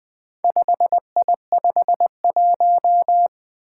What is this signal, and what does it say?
Morse code, heard as a keyed tone, about 20 wpm, 700 Hz. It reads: 5I51
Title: Four Challenging Letter-Number Combinations 20wpm